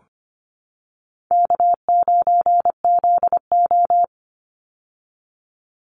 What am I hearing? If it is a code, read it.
Morse code, heard as a keyed tone, about 25 wpm, 700 Hz. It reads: K9ZO
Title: Easy Call Signs Repeated 3x - 25wpm